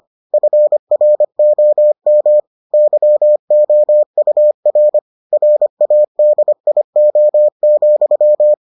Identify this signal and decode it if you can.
Morse code, heard as a keyed tone, about 25 wpm, 600 Hz. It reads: FROM YOUR RADIO,